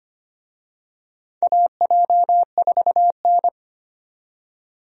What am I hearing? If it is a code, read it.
Morse code, heard as a keyed tone, about 25 wpm, 700 Hz. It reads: AJ4N
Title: USA Call Signs - 4 Characters - 25wpm